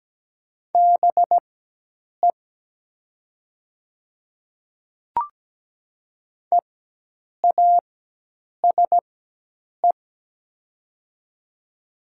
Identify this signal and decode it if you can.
Morse code, heard as a keyed tone, about 17 wpm, spaced as for 8 wpm, 700 Hz. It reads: BE EASE
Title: 4x Character Spacing - Top 300 Words - 17wpm